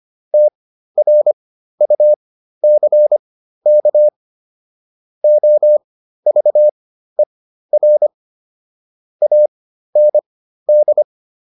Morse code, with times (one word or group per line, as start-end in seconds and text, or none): 0.3-4.1 TRUCK
5.2-8.1 OVER
9.2-11.0 AND